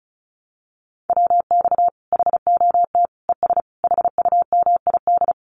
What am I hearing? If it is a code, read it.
Morse code, heard as a keyed tone, about 34 wpm, 700 Hz. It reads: WX HOT ES HUMID
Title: Common QSO Phrases 35wpm